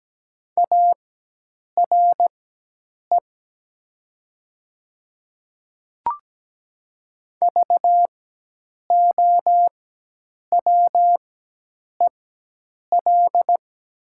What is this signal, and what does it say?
Morse code, heard as a keyed tone, about 17 wpm, spaced as for 8 wpm, 700 Hz. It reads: ARE VOWEL
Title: 4x Character Spacing - Top 500 Words - 17wpm